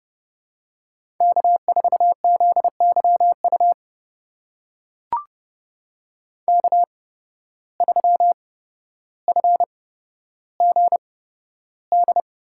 Morse code, heard as a keyed tone, about 30 wpm, spaced as for 13 wpm, 700 Hz. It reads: K4ZYU K3FGD
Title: USA Call Signs - 5 Characters - 8x Character Spacing - 30wpm